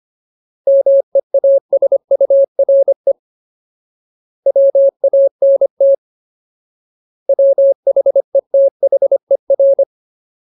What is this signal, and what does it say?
Morse code, heard as a keyed tone, about 25 wpm, 550 Hz. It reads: MEASURE WANT WHETHER
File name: Ultimate Word List / 25WPM Ultimate